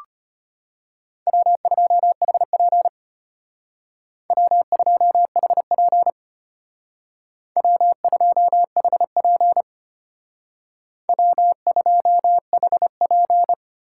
Morse code, heard as a keyed tone, about 31 wpm, 700 Hz. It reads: W2HP W2HP W2HP W2HP
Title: Speed Racing - USA Call Signs 25wpm